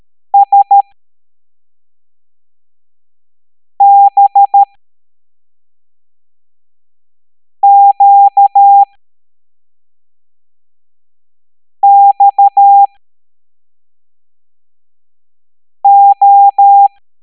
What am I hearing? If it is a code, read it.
Morse code, heard as a keyed tone, about 13 wpm, spaced as for 3 wpm, 800 Hz. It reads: SBQXO